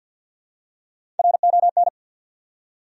Morse code, heard as a keyed tone, about 48 wpm, 700 Hz. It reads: RON
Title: USA Common Names - Max 5 Letters - 50wpm